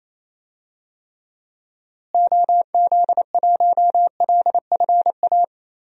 Morse code, heard as a keyed tone, about 27 wpm, 700 Hz. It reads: OZ1LFA
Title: DXCC-100 Call Signs - 6 Characters - Repeated 3x - 28wpm